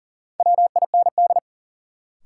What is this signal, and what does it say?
Morse code, heard as a keyed tone, about 40 wpm, 700 Hz. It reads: WIND
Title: Sentences from Top 400 Words - New Word Drill - 40wpm